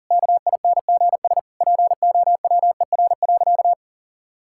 Morse code, heard as a keyed tone, about 40 wpm, 700 Hz. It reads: KINGS POWER.
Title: Sentences from Top 1000 Words - Review Part 6 - 40wpm